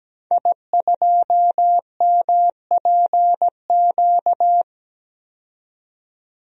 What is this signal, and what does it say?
Morse code, heard as a keyed tone, about 17 wpm, 700 Hz. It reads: I2MPQ
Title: DXCC-100 Call Signs - 5 Characters - 17wpm